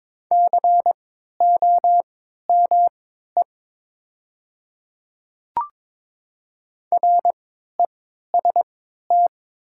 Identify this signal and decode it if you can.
Morse code, heard as a keyed tone, about 21 wpm, spaced as for 12 wpm, 700 Hz. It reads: COME REST
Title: 3x Character Spacing - Top 500 Words - 22wpm